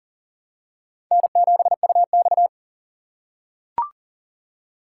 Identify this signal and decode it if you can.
Morse code, heard as a keyed tone, about 40 wpm, 700 Hz. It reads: N7UX
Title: USA Call Signs - 2x Character Spacing - 40wpm